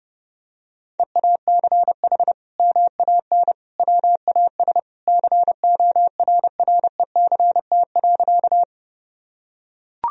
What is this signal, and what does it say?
Morse code, heard as a keyed tone, about 30 wpm, 700 Hz. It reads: EACH MAN WAS CORRECT.
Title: Sentences from Top 1000 Words - Review Part 2 - 30wpm